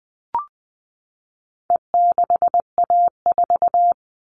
Morse code, heard as a keyed tone, about 20 wpm, 700 Hz. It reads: E6A4
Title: All Cut Numbers - 4-Digit 20wpm